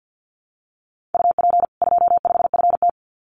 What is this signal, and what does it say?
Morse code, heard as a keyed tone, about 50 wpm, 700 Hz. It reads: UP 25FT
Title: Common QSO Phrases 50wpm